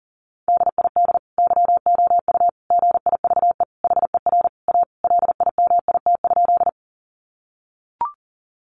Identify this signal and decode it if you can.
Morse code, heard as a keyed tone, about 40 wpm, 700 Hz. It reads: DID YOU GIVE HER A LIMIT?
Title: Sentences from Top 1500 Words 40wpm